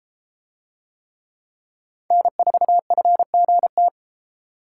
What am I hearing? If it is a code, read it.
Morse code, heard as a keyed tone, about 31 wpm, 700 Hz. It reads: N4FGT